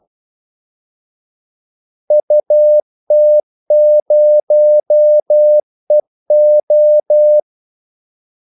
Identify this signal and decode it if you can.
Morse code, heard as a keyed tone, about 12 wpm, 600 Hz. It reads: UT0EO
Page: 2 ekstra ordmellemrum mellem kaldesignalerne